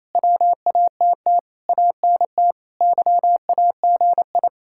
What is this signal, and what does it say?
Morse code, heard as a keyed tone, about 37 wpm, 700 Hz. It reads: WATT ANT YAGI